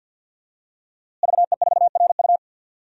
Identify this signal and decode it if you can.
Morse code, heard as a keyed tone, about 52 wpm, 700 Hz. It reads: VE4RU
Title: DXCC-100 Call Signs - 5 Characters - Repeated 3x - 50wpm